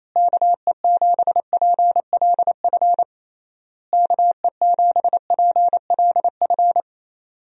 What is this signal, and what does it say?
Morse code, heard as a keyed tone, about 28 wpm, 700 Hz. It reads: KE7PLF KE7PLF
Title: USA Call Signs - 6 Characters - Repeated 3x - 28wpm